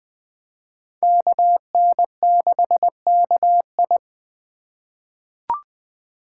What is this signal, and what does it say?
Morse code, heard as a keyed tone, about 20 wpm, 700 Hz. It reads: KN6KI
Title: 2023 POTA Hunter Call Signs - 20wpm